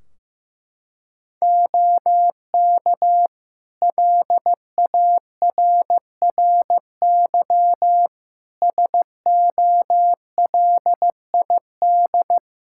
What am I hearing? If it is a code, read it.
Morse code, heard as a keyed tone, about 15 wpm, 700 Hz. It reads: OK LARRY SOLID
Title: Practice QSO - Exchange 3 - More Info - 15wpm